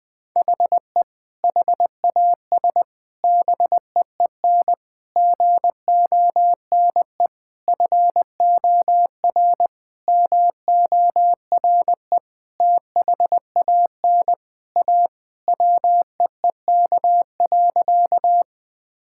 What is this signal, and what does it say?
Morse code, heard as a keyed tone, about 20 wpm, 700 Hz. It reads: HE HAS BEEN GONE FOR MORE THAN A WEEK.
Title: Sentences with 4-Word Phrases - Spoken Beginning - 20wpm